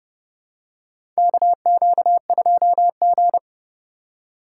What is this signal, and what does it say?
Morse code, heard as a keyed tone, about 30 wpm, 700 Hz. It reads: KQ2G
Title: USA Call Signs - 4 Characters - 3x Character Spacing - 30wpm